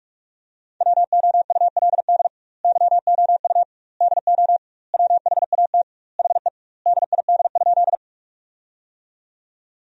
Morse code, heard as a keyed tone, about 46 wpm, 700 Hz. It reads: WOULD YOU DO WHAT HE DID?
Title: 2x Word-Spacing - Sentences from Top 100 Words - 45wpm